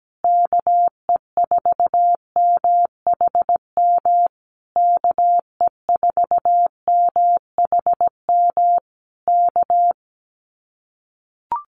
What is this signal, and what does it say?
Morse code, heard as a keyed tone, about 17 wpm, 700 Hz. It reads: KE4MHM KE4MHM K
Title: Common QSO Phrases Encore 17wpm